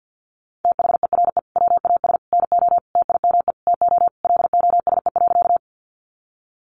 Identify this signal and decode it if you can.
Morse code, heard as a keyed tone, about 50 wpm, 700 Hz. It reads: THERE WAS NO TIME TO LOSE.
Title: Sentences from Top 1100 Words 50wpm